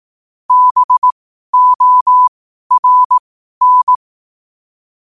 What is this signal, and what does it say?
Morse code, heard as a keyed tone, about 18 wpm, spaced as for 12 wpm, 1000 Hz. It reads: BORN
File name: qotdfiles / 12wpm